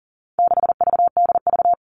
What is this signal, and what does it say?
Morse code, heard as a keyed tone, about 40 wpm, 700 Hz. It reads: 6VDV
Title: All Cut Numbers - 4-Digit 40wpm